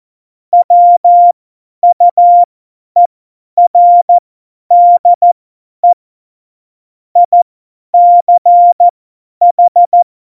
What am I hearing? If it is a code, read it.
Morse code, heard as a keyed tone, about 14 wpm, spaced as for 10 wpm, 700 Hz. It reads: WUERDE ICH
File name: Morsekurs / Witze / Witz 22. Mai 2018, 50 F70